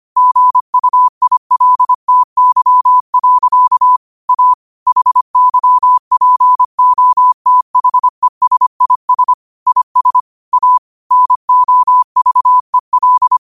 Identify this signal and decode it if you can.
Morse code, heard as a keyed tone, about 25 wpm, 1000 Hz. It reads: GUILTY. A HYPOTHESIS IS A NOVEL
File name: qotdfiles / 25wpm